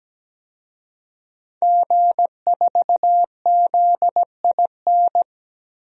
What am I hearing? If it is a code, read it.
Morse code, heard as a keyed tone, about 17 wpm, 700 Hz. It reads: G4ZIN
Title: DXCC-100 Call Signs - 5 Characters - Repeated 2x - 17wpm